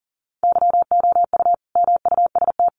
Morse code, heard as a keyed tone, about 40 wpm, 700 Hz. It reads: YOU MUST
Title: Sentences from Top 1400 Words 40wpm